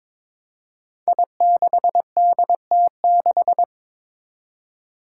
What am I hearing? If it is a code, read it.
Morse code, heard as a keyed tone, about 22 wpm, 700 Hz. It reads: I6DT6
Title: Five Challenging Letter-Number Combinations 22wpm